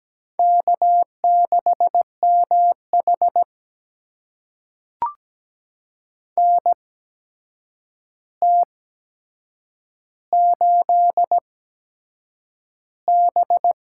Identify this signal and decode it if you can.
Morse code, heard as a keyed tone, about 17 wpm, spaced as for 7 wpm, 700 Hz. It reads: K6MH NT8B
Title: USA Call Signs - 4 Characters - 8x Character Spacing - 17wpm